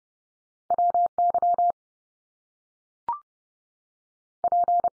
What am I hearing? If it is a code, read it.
Morse code, heard as a keyed tone, about 30 wpm, 700 Hz. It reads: WY P
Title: US States 30WPM